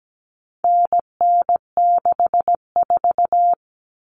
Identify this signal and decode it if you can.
Morse code, heard as a keyed tone, about 17 wpm, 700 Hz. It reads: NN64
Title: Common Cut Numbers - 4-Digit 17wpm